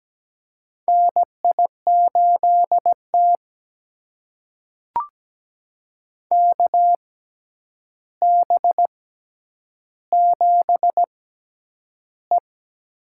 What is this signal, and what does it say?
Morse code, heard as a keyed tone, about 17 wpm, spaced as for 9 wpm, 700 Hz. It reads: NI8T KB7E
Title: USA Call Signs - 4 Characters - 6x Character Spacing - 17wpm